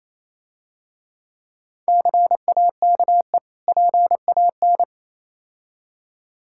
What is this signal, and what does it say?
Morse code, heard as a keyed tone, about 27 wpm, 700 Hz. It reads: CAKE PAN
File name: Easy Words - Speed-Racing with Hints - 22wpm